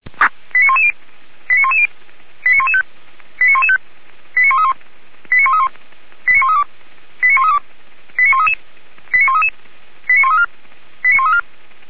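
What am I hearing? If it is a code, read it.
Five-tone selective call (ZVEI), heard as quick runs of five tones, tones of about 70 ms. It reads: 89110 89110 89115 89115 89131 89131 89132 89132 89133 89133 89134 89134
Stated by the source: ZVEI five-tone call received by SDR, Feuerwehr Orlamünde, channel 464.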